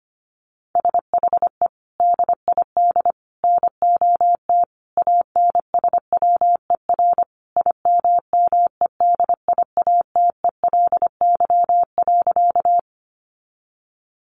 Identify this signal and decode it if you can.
Morse code, heard as a keyed tone, about 25 wpm, 700 Hz. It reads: SHE DID NOT ANSWER IMMEDIATELY.
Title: Sentences from Top 1100 Words 25wpm